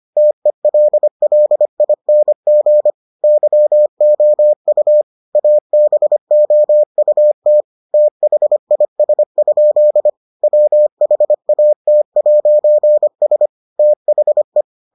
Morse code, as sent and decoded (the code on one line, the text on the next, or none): - . .-.. .-.. .. -. --. / -.-- --- ..- / .- -... --- ..- - / - .... .. ... ..--.. / .-- .... .- - .----. ... / - .... .
TELLING YOU ABOUT THIS? WHAT'S THE